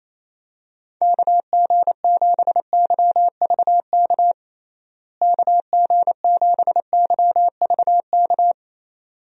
Morse code, heard as a keyed tone, about 28 wpm, 700 Hz. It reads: KG7YVK KG7YVK
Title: USA Call Signs - 6 Characters - Repeated 2x - 28wpm